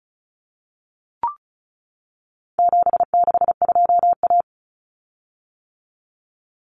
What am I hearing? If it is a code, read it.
Morse code, heard as a keyed tone, about 34 wpm, 700 Hz. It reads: 762A
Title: Common Cut Numbers - 4-Digit 35wpm